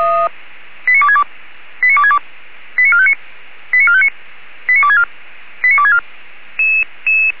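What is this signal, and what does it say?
Five-tone selective call (ZVEI), heard as quick runs of five tones, tones of about 70 ms. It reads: 89262 89262 89459 89459 89264 89264
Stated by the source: ZVEI five-tone selective call received on SDR.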